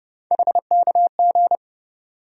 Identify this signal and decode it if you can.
Morse code, heard as a keyed tone, about 32 wpm, 700 Hz. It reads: HKG
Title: USA Call Sign Postfixes - 3 Letters 30wpm